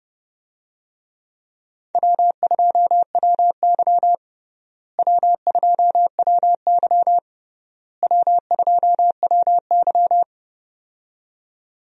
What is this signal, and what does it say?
Morse code, heard as a keyed tone, about 30 wpm, 700 Hz. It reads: W2WY W2WY W2WY
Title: USA Call Signs - 4 Characters - Repeated 3x - 30wpm